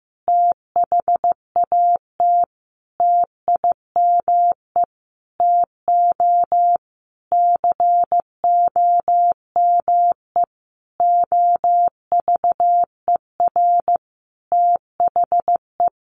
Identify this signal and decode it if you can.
Morse code, heard as a keyed tone, about 15 wpm, 700 Hz. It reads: THAT TIME TO COME OVER THE
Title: Sentences from Top 300 Words 15wpm